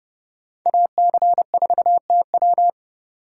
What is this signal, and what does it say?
Morse code, heard as a keyed tone, about 30 wpm, 700 Hz. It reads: AC4TW